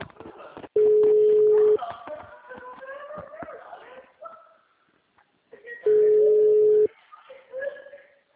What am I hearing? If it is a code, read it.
Ringback tone (Europe, 425 Hz), with speech.
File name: out- -2027- - - .30535